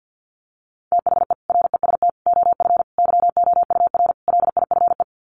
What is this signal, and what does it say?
Morse code, heard as a keyed tone, about 50 wpm, 700 Hz. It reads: THE REST OF YOUR LIFE
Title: Sets of 5 Words 50wpm